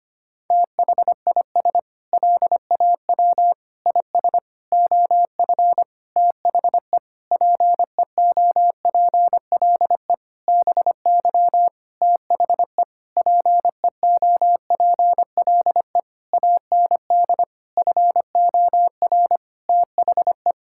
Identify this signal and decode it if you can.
Morse code, heard as a keyed tone, about 25 wpm, 700 Hz. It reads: THIS LAW IS OF THE PEOPLE BY THE PEOPLE AND FOR THE
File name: practice / 25wpm / Sentences with 9-Word Phrases - 25wpm